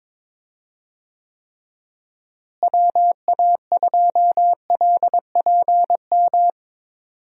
Morse code, heard as a keyed tone, about 22 wpm, 700 Hz. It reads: WA2LPM
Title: USA Call Signs - 6 Characters - Repeated 3x - 22wpm